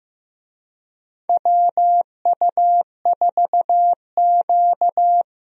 Morse code, heard as a keyed tone, about 15 wpm, 700 Hz. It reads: WU4Q
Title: USA Call Signs - 4 Characters - 15wpm